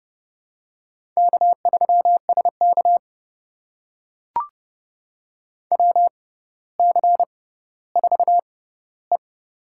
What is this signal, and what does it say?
Morse code, heard as a keyed tone, about 30 wpm, spaced as for 15 wpm, 700 Hz. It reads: K3SK WC4E
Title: USA Call Signs - 4 Characters - 6x Character Spacing - 30wpm